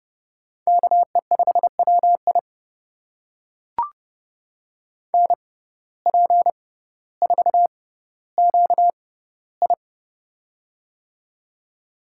Morse code, heard as a keyed tone, about 30 wpm, spaced as for 15 wpm, 700 Hz. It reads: KE5WI NP4QI
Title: USA Call Signs - 5 Characters - 6x Character Spacing - 30wpm